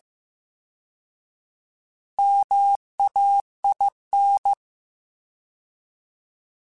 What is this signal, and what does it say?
Morse code, heard as a keyed tone, about 15 wpm, 800 Hz. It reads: MAIN
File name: Alphabetical / Alphabetical mp3s / Ms @13